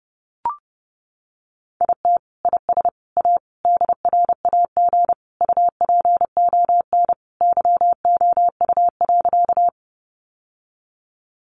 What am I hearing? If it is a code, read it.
Morse code, heard as a keyed tone, about 30 wpm, 700 Hz. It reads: IT IS A DRAG UPON YOU.
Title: Sentences from Top 1500 Words 30wpm